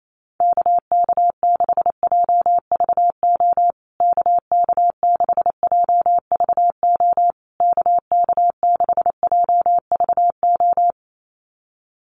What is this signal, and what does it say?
Morse code, heard as a keyed tone, about 28 wpm, 700 Hz. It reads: KK6JVO KK6JVO KK6JVO
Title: USA Call Signs Repeated 3x 28wpm